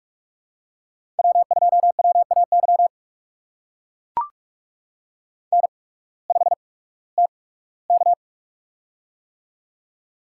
Morse code, heard as a keyed tone, about 44 wpm, spaced as for 21 wpm, 700 Hz. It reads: W2WAY N5TK
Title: USA Call Signs - 8x Character Spacing - 45wpm